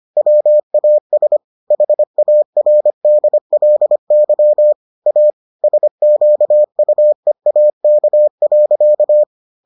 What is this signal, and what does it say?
Morse code, heard as a keyed tone, about 25 wpm, 600 Hz. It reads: WAS HARDLY A SQUEAK.